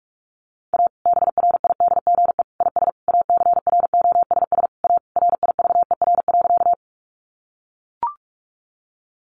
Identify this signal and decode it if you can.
Morse code, heard as a keyed tone, about 44 wpm, 700 Hz. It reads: A BRIDGE IS ACROSS A RIVER.
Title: Sentences from Top 1200 Words 45wpm